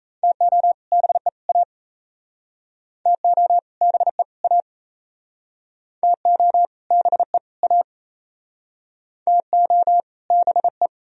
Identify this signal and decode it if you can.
Morse code, heard as a keyed tone, about 37 wpm, 700 Hz. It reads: TO BE A TO BE A TO BE A TO BE